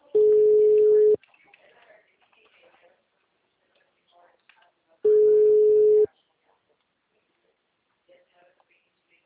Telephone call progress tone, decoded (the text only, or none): ringback tone